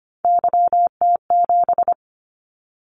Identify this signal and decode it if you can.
Morse code, heard as a keyed tone, about 25 wpm, 700 Hz. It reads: YT7
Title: Callsign Prefixes for Top 25 DXCC - 25wpm